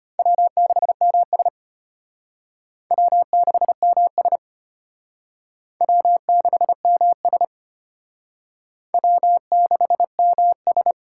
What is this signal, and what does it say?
Morse code, heard as a keyed tone, about 31 wpm, 700 Hz. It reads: W6MS W6MS W6MS W6MS
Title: Speed Racing - USA Call Signs 25wpm